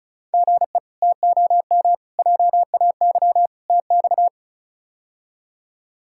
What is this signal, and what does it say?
Morse code, heard as a keyed tone, about 34 wpm, 700 Hz. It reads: GE TOM JAY TX
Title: K1USN SST QSO Practice - 35wpm